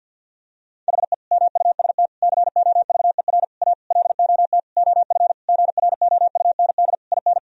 Morse code, heard as a keyed tone, about 50 wpm, 700 Hz. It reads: HE MUST COVER A LOT OF GROUND IN
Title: Sentences from Top 1000 Words - Review Part 3 - 50wpm